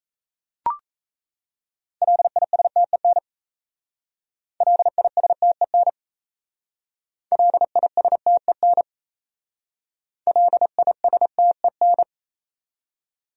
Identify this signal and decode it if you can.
Morse code, heard as a keyed tone, about 34 wpm, 700 Hz. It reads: LISTEN LISTEN LISTEN LISTEN
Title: Speed-Racing Top 500 Words 28wpm